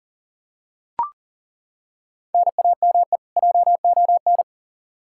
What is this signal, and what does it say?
Morse code, heard as a keyed tone, about 40 wpm, 700 Hz. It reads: NAME JON
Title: K1USN SST QSO Practice 40wpm